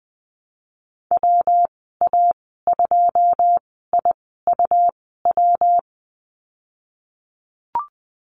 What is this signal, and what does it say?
Morse code, heard as a keyed tone, about 20 wpm, spaced as for 15 wpm, 700 Hz. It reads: WA2IUW
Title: 122 Lesson 20g - Number 3 - Call Signs 20wpm